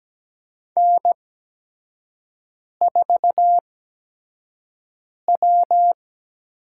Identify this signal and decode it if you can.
Morse code, heard as a keyed tone, about 17 wpm, 700 Hz. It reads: N 4 W
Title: USA Call Signs - 8x Character Spacing - 17wpm